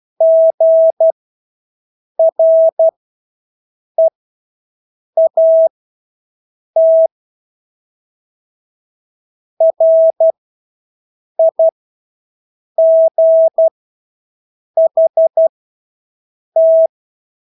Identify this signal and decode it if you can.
Morse code, heard as a keyed tone, about 12 wpm, spaced as for 6 wpm, 650 Hz. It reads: GREAT RIGHT
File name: Engelske ord 12-6 wpm